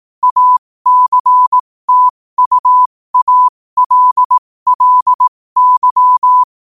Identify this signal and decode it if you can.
Morse code, heard as a keyed tone, about 18 wpm, spaced as for 15 wpm, 1000 Hz. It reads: ACTUALLY
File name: qotdfiles / 15wpm